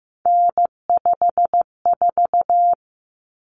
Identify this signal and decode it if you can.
Morse code, heard as a keyed tone, about 15 wpm, 700 Hz. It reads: N54
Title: Common Cut Numbers - 3-Digit 15wpm